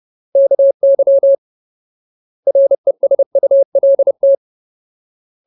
Morse code, heard as a keyed tone, about 30 wpm, 550 Hz. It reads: KY RESULT
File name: Ultimate Word List / 30WPM Ultimate